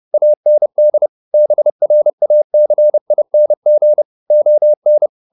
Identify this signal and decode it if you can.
Morse code, heard as a keyed tone, about 30 wpm, 600 Hz. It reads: AND BRACING ON
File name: the old man and the sea / the old man and the sea 1 30wpm